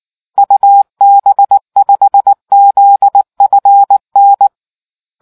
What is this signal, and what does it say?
Morse code, heard as a keyed tone, about 19 wpm, 800 Hz. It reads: UB5ZFN